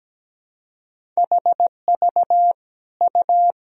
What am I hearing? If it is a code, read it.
Morse code, heard as a keyed tone, about 17 wpm, 700 Hz. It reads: HV U
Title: Sentences from Top 500 Words and Abbreviations 17wpm